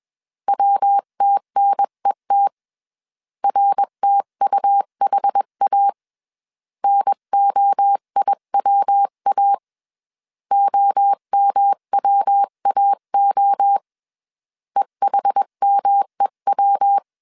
Morse code, heard as a keyed tone, about 21 wpm, 800 Hz. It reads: WTNET RTUHA NOIWA OMWAO EHMEW
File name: 20wpm